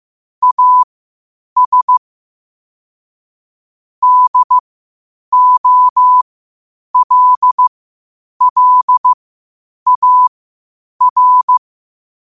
Morse code, heard as a keyed tone, about 15 wpm, spaced as for 8 wpm, 1000 Hz. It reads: AS DOLLAR